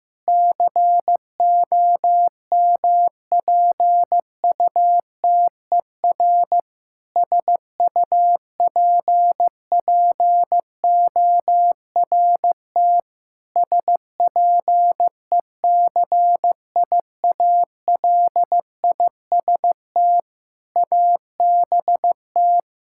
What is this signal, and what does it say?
Morse code, heard as a keyed tone, about 15 wpm, 700 Hz. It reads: COMPUTER SUPPORT SPECIALIST ABT
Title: Practice QSO - Exchange 3 - More Info - 15wpm